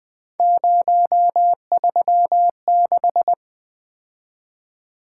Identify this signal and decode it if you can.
Morse code, heard as a keyed tone, about 24 wpm, 700 Hz. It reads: 036
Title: Speed-Racing - Three Letter-Number Combinations - 20wpm